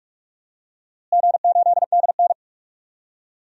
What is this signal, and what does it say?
Morse code, heard as a keyed tone, about 46 wpm, 700 Hz. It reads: G8DN
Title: DXCC-100 Call Signs - 4 Characters - Repeated 3x - 45wpm